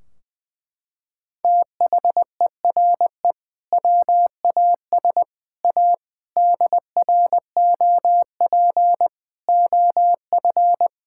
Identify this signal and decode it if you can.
Morse code, heard as a keyed tone, about 20 wpm, 700 Hz. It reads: THERE WAS A DROP OF
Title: Sentences from Top 1600 Words 20wpm